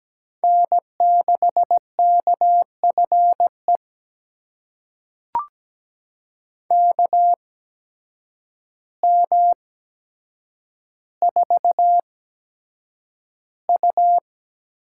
Morse code, heard as a keyed tone, about 17 wpm, spaced as for 8 wpm, 700 Hz. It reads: N6KFE KM4U
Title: USA Call Signs - 8x Character Spacing - 17wpm